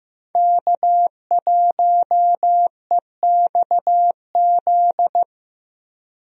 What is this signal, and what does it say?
Morse code, heard as a keyed tone, about 15 wpm, 700 Hz. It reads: K1EXZ
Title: USA Call Signs - 5 Characters - Repeated 3x - 15wpm